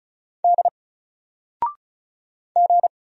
Morse code, heard as a keyed tone, about 36 wpm, 700 Hz. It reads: D G
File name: Single-Letter-Number-35wpm